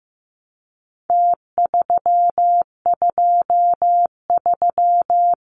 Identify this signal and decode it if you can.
Morse code, heard as a keyed tone, about 15 wpm, 700 Hz. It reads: T323
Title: Numbers - Contest Style - 4-Digit 15wpm